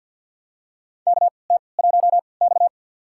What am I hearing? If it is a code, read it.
Morse code, heard as a keyed tone, about 53 wpm, spaced as for 29 wpm, 700 Hz. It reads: KT1X
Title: USA Call Signs - 4 Characters - 3x Character Spacing - 50wpm